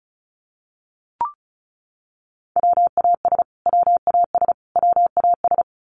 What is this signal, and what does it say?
Morse code, heard as a keyed tone, about 34 wpm, 700 Hz. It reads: WAS WAS WAS
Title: Top 200 Words Repeated 35wpm